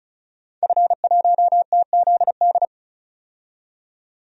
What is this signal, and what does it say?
Morse code, heard as a keyed tone, about 36 wpm, 700 Hz. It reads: F1TZD
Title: DXCC-100 Call Signs - 5 Characters - 35wpm